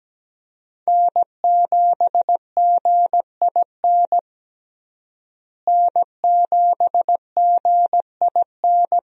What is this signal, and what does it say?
Morse code, heard as a keyed tone, about 17 wpm, 700 Hz. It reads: N7GIN N7GIN
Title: USA Call Signs - 5 Characters - Repeated 3x - 17wpm